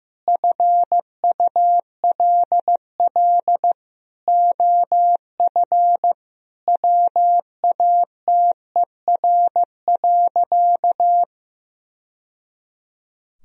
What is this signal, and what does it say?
Morse code, heard as a keyed tone, about 15 wpm, 700 Hz. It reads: FULL OF WATER.
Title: Sentences from Top 1000 Words - Review Part 3 - 15wpm